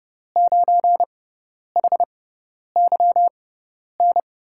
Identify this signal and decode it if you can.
Morse code, heard as a keyed tone, about 30 wpm, spaced as for 10 wpm, 700 Hz. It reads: 9HYN
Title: USA Call Signs - 6 Characters - 6x Character Spacing - 30wpm